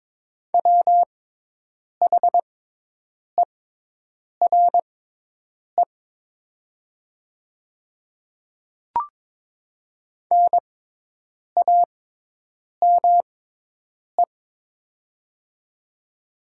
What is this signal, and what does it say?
Morse code, heard as a keyed tone, about 22 wpm, spaced as for 8 wpm, 700 Hz. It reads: WHERE NAME
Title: 6x Character Spacing - Top 200 Words - 22wpm